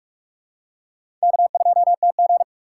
Morse code, heard as a keyed tone, about 46 wpm, 700 Hz. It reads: K2TG
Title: USA Call Signs - 4x Character Spacing - 45wpm